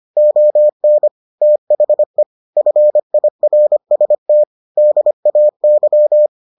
Morse code, heard as a keyed tone, about 25 wpm, 600 Hz. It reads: ON THE FIRST DAY